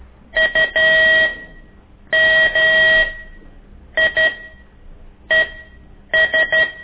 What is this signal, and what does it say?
Morse code, heard as a keyed tone, about 12 wpm, spaced as for 7 wpm, 1900 Hz. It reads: UMIES
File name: ik1weg2 / CORSI CW